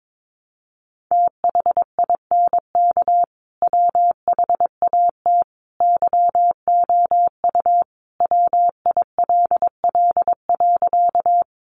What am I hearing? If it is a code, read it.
Morse code, heard as a keyed tone, about 22 wpm, 700 Hz. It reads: THINK WHAT YOU WILL.
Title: Sentences from Top 200 Words 22wpm